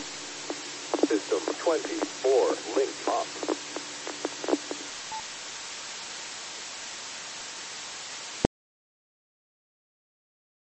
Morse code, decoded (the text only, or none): none